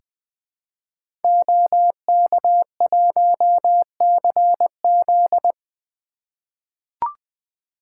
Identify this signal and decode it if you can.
Morse code, heard as a keyed tone, about 20 wpm, 700 Hz. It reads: OK1CZ
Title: 2023 SOTA Activator Call Signs - 20wpm